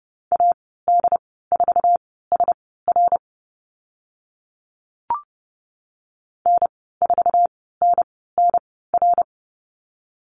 Morse code, heard as a keyed tone, about 30 wpm, spaced as for 17 wpm, 700 Hz. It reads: AD4SR N4NNR
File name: Lesson 15g - Number 5 - Call Signs 30wpm